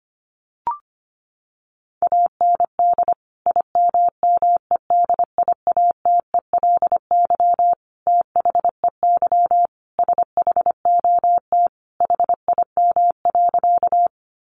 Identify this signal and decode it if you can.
Morse code, heard as a keyed tone, about 25 wpm, 700 Hz. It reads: AND IMMEDIATELY THEY SHOT HIM.
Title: Sentences from Top 1100 Words 25wpm